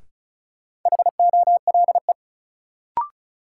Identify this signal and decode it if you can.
Morse code, heard as a keyed tone, about 36 wpm, 700 Hz. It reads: HOLE
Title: Sentences from Top 1600 Words - New Word Drill Alt - 35wpm